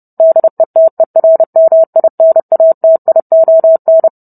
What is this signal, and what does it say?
Morse code, heard as a keyed tone, about 30 wpm, 650 Hz. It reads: DETERMINATION